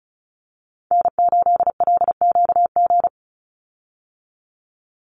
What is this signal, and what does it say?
Morse code, heard as a keyed tone, about 34 wpm, 700 Hz. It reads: N8LQG
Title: USA Call Signs 35wpm